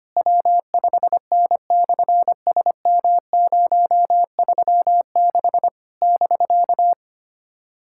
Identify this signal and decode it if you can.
Morse code, heard as a keyed tone, about 27 wpm, 700 Hz. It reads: W5N/SM036 <BK>
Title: SOTA QSO Practice 25wpm